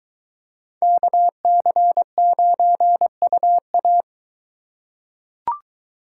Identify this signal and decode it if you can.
Morse code, heard as a keyed tone, about 22 wpm, 700 Hz. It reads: KC9UA